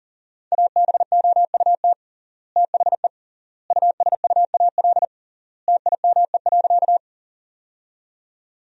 Morse code, heard as a keyed tone, about 40 wpm, 700 Hz. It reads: ABOUT THE USUAL TIME.
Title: 3x Word-Spacing - Sentences from Top 400 Words - 40wpm